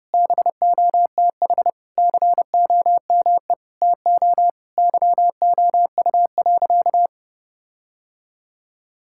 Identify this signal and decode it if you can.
Morse code, heard as a keyed tone, about 30 wpm, 700 Hz. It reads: BOTH COME TO YOU.
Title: Sentences from Top 1600 Words 30wpm